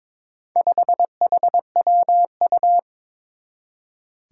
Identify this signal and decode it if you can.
Morse code, heard as a keyed tone, about 22 wpm, 700 Hz. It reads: 5HWU
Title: Four Challenging Letter-Number Combinations 22wpm